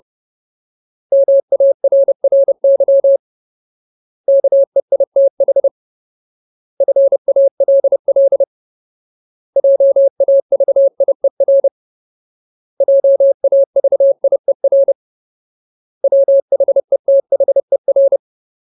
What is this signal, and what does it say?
Morse code, heard as a keyed tone, about 30 wpm, 550 Hz. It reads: MARRY KEITH FALL JAVIER JAVIER WHETHER
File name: Ultimate Word List / 30WPM Ultimate